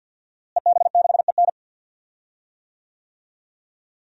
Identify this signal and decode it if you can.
Morse code, heard as a keyed tone, about 52 wpm, 700 Hz. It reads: EB6EN
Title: DXCC-100 Call Signs - 5 Characters - Repeated 2x - 50wpm